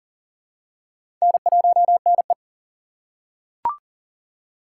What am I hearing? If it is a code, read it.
Morse code, heard as a keyed tone, about 40 wpm, 700 Hz. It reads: N1NE
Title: USA Call Signs - 4 Characters - 2x Character Spacing - 40wpm